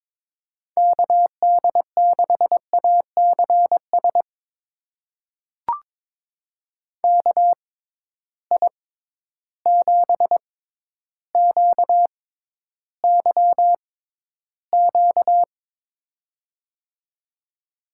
Morse code, heard as a keyed tone, about 22 wpm, spaced as for 11 wpm, 700 Hz. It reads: KD6ACS KI7QYQ
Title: USA Call Signs - 6 Characters - 6x Character Spacing - 22wpm